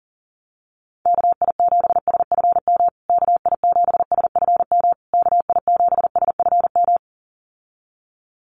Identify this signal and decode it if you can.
Morse code, heard as a keyed tone, about 37 wpm, 700 Hz. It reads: KI7SFM KI7SFM KI7SFM
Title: USA Call Signs Repeated 3x 40wpm